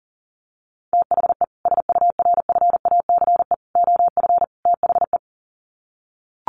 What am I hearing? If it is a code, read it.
Morse code, heard as a keyed tone, about 40 wpm, 700 Hz. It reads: THE SURFACE OF THE
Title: Sets of 4 Words 40wpm